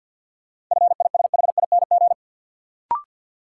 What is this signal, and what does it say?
Morse code, heard as a keyed tone, about 51 wpm, 700 Hz. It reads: FISHING
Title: Sentences from Top 1600 Words - New Word Drill Alt - 50wpm